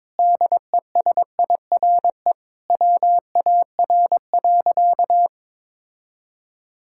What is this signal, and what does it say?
Morse code, heard as a keyed tone, about 22 wpm, 700 Hz. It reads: DESIRE WAR.
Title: Sentences from Top 1600 Words 22wpm